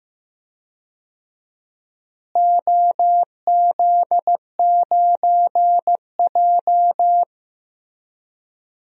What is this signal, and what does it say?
Morse code, heard as a keyed tone, about 15 wpm, 700 Hz. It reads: OZ9J
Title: DXCC-100 Call Signs - 4 Characters - Repeated 2x - 15wpm